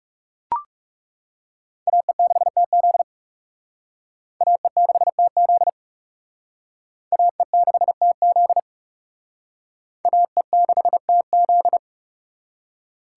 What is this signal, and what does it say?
Morse code, heard as a keyed tone, about 35 wpm, 700 Hz. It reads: AE6TZ AE6TZ AE6TZ AE6TZ
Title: Speed Racing - USA 2x2 Call Signs 30wpm